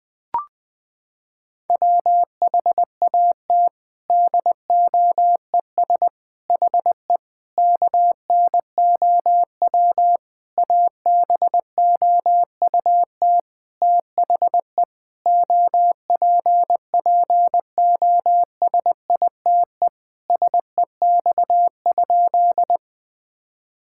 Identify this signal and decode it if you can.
Morse code, heard as a keyed tone, about 21 wpm, 700 Hz. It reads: WHAT DOES HE KNOW ABOUT THE OPPOSITE SEX?
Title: Sentences from Top 1600 Words 20wpm